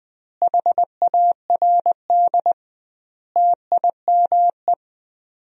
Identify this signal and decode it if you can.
Morse code, heard as a keyed tone, about 20 wpm, 700 Hz. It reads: HARD TIME
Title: 2x Word-Spacing - Sentences from Top 300 Words - 20wpm